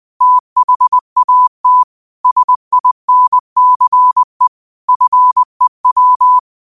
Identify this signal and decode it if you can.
Morse code, heard as a keyed tone, about 20 wpm, 1000 Hz. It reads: THAT SINCE FEW